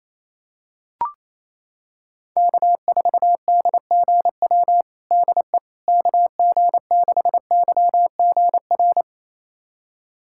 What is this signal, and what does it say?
Morse code, heard as a keyed tone, about 28 wpm, 700 Hz. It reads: K4DGW DE KG6YGR
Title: USA Callsign DE Callsign 28wpm